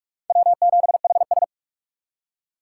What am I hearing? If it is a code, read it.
Morse code, heard as a keyed tone, about 57 wpm, 700 Hz. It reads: W7HS